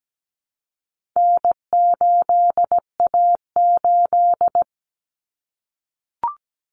Morse code, heard as a keyed tone, about 17 wpm, 700 Hz. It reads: N8A8
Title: Common Cut Numbers - 4-Digit 17wpm